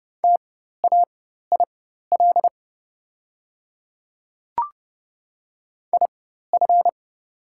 Morse code, heard as a keyed tone, about 30 wpm, spaced as for 14 wpm, 700 Hz. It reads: TAIL IF
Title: 4x Character Spacing - Top 500 Words - 30wpm